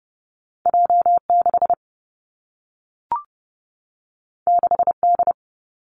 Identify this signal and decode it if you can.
Morse code, heard as a keyed tone, about 30 wpm, 700 Hz. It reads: J 6 6 D
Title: Two Challenging Letter-Number Combinations 30wpm